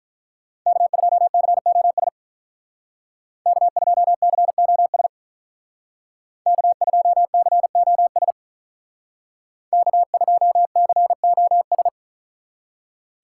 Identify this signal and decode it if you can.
Morse code, heard as a keyed tone, about 45 wpm, 700 Hz. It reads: K2COS K2COS K2COS K2COS